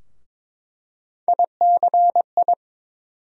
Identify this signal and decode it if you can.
Morse code, heard as a keyed tone, about 22 wpm, 700 Hz. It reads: ICI
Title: USA Call Sign Postfixes - 3 Letters 22wpm